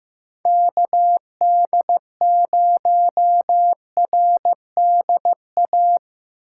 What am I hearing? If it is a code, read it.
Morse code, heard as a keyed tone, about 15 wpm, 700 Hz. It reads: KD0RDA
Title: USA Call Signs - 6 Characters - 3x Character Spacing - 15wpm